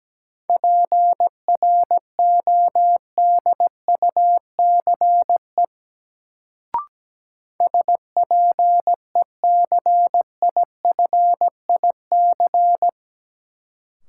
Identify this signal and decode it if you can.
Morse code, heard as a keyed tone, about 17 wpm, 700 Hz. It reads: PRODUCE SPECIFIC
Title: Sentences from Top 1000 Words - Review Part 3 - New Word Drill - 17wpm